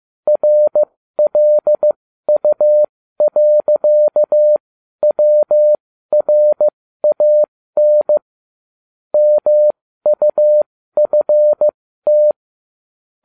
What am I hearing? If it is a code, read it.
Morse code, heard as a keyed tone, about 15 wpm, spaced as for 12 wpm, 600 Hz. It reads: RLU.WRAN MUFT
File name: cw / ubacw20 3